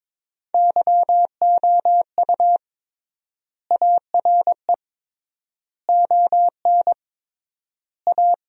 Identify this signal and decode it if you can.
Morse code, heard as a keyed tone, about 22 wpm, 700 Hz. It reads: YOU ARE ON A
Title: 3x Word-Spacing - Sentences from Top 400 Words - 22wpm